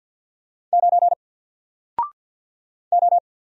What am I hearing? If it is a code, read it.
Morse code, heard as a keyed tone, about 48 wpm, 700 Hz. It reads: 9 O
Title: Single Character (Letter:Number:Punctuation:Prosign) Alt 50wpm